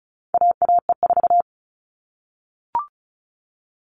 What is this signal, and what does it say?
Morse code, heard as a keyed tone, about 34 wpm, 700 Hz. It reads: AAE4
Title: All Cut Numbers - 4-Digit 35wpm